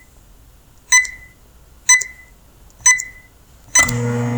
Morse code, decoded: none